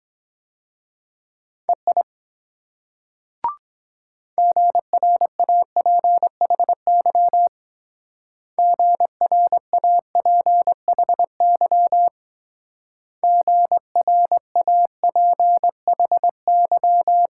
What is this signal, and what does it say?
Morse code, heard as a keyed tone, about 22 wpm, 700 Hz. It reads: EI GRAPHY GRAPHY GRAPHY